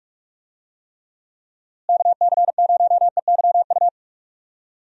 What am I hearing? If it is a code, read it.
Morse code, heard as a keyed tone, about 41 wpm, 700 Hz. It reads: KC0EYU
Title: USA Call Signs - 6 Characters - Repeated 3x - 45wpm